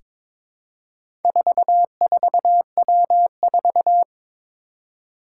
Morse code, heard as a keyed tone, about 22 wpm, 700 Hz. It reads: 44W4
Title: Four Challenging Letter-Number Combinations 22wpm